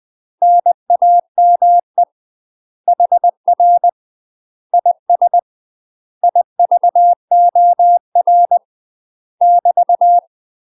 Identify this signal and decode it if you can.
Morse code, heard as a keyed tone, about 20 wpm, 700 Hz. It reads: NAME HR IS IVOR <BT>
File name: UK MASTER QSOs TRAINING / 20WPM / UK MASTER QSOs 0004